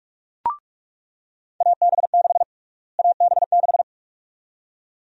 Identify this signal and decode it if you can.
Morse code, heard as a keyed tone, about 46 wpm, 700 Hz. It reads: AB6 AB6
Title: USA Call Sign Prefixes - Repeated 2x - 45wpm